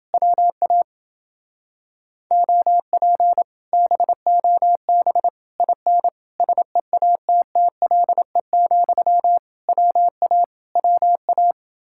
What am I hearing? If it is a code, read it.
Morse code, heard as a keyed tone, about 28 wpm, 700 Hz. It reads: WA OP BOB IN SEATTLE, WA WA